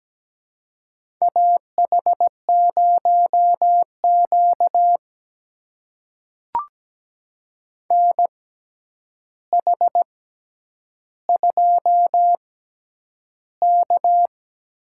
Morse code, heard as a keyed tone, about 17 wpm, spaced as for 9 wpm, 700 Hz. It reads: AH0Q NH2K
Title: USA Call Signs - 4 Characters - 6x Character Spacing - 17wpm